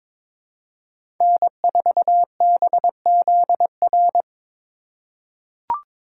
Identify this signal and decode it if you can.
Morse code, heard as a keyed tone, about 22 wpm, 700 Hz. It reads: N4BZR
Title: USA Call Signs - 3x Character Spacing - 22wpm